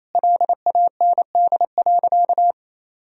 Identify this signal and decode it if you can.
Morse code, heard as a keyed tone, about 27 wpm, 700 Hz. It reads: LAND.
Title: Sentences from Top 1000 Words - Review Part 2 - 28wpm